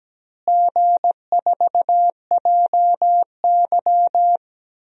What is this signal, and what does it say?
Morse code, heard as a keyed tone, about 17 wpm, 700 Hz. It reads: G4JY
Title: DXCC-100 Call Signs - 4 Characters - Repeated 3x - 17wpm